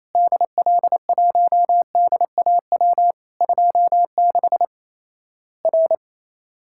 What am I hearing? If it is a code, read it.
Morse code, heard as a keyed tone, about 28 wpm, 700 Hz. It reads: DL1DAW 26 R
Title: Worked All Europe DX Contest - QTC Exchange - 28wpm